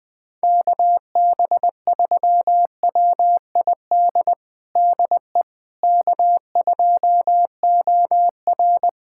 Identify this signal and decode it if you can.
Morse code, heard as a keyed tone, about 21 wpm, 700 Hz. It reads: KB3WID DE K2OR
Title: USA Callsign DE Callsign 20wpm